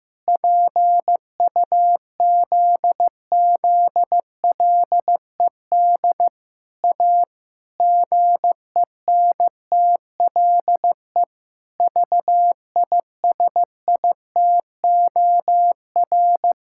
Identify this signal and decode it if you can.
Morse code, heard as a keyed tone, about 15 wpm, 700 Hz. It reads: PUZZLED A GENTLE VISITOR
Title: Pangram Challenge v2 - 15wpm